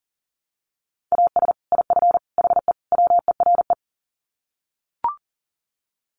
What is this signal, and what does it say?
Morse code, heard as a keyed tone, about 40 wpm, 700 Hz. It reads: AS IF HE WERE
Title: Sets of 4 Words 40wpm